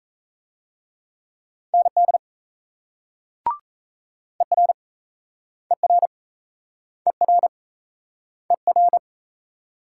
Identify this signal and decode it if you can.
Morse code, heard as a keyed tone, about 36 wpm, 700 Hz. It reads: ND ER ER ER ER